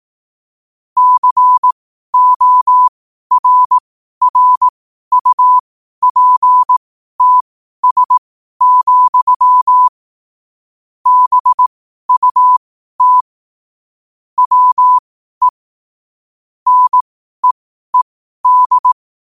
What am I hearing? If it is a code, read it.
Morse code, heard as a keyed tone, about 18 wpm, spaced as for 12 wpm, 1000 Hz. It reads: CORRUPTS, BUT WE NEED
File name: qotdfiles / 12wpm